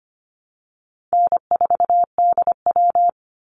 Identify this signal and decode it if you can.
Morse code, heard as a keyed tone, about 25 wpm, 700 Hz. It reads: N4DW
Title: Easy Call Signs Repeated 3x - 25wpm